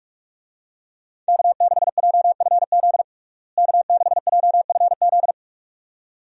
Morse code, heard as a keyed tone, about 46 wpm, 700 Hz. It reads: K6JFZ K6JFZ
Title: USA Call Signs - 5 Characters - Repeated 2x - 45wpm